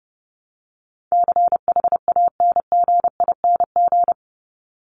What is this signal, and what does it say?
Morse code, heard as a keyed tone, about 30 wpm, 700 Hz. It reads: CHANGING
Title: ING Suffix - Top 100 Words with Base Words 30wpm